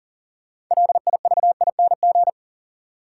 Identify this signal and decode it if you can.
Morse code, heard as a keyed tone, about 39 wpm, 700 Hz. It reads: LIVING
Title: Sets of 4 Words - New Word Drill - 40wpm